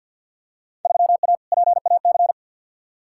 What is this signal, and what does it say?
Morse code, heard as a keyed tone, about 48 wpm, 700 Hz. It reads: 3A PAC
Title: Field Day Exchanges Encore - 50wpm